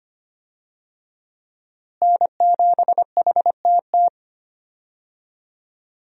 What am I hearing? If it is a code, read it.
Morse code, heard as a keyed tone, about 25 wpm, 700 Hz. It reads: N7HTT
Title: USA Call Signs - 5 Characters - Repeated 2x - 25wpm